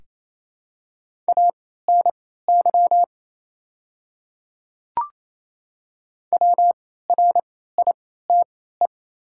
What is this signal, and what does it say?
Morse code, heard as a keyed tone, about 28 wpm, spaced as for 16 wpm, 700 Hz. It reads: ANY WRITE
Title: 3x Character Spacing - Top 300 Words - 28wpm